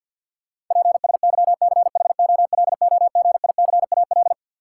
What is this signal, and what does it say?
Morse code, heard as a keyed tone, about 53 wpm, 700 Hz. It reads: PSYCHOLOGICAL
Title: Sentences from Top 1400 Words - New Word Drill Alt - 50wpm